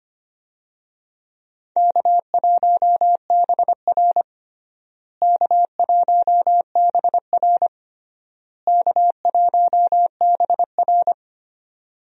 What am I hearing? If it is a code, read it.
Morse code, heard as a keyed tone, about 25 wpm, 700 Hz. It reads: K1BR K1BR K1BR
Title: USA Call Signs - 4 Characters - Repeated 3x - 25wpm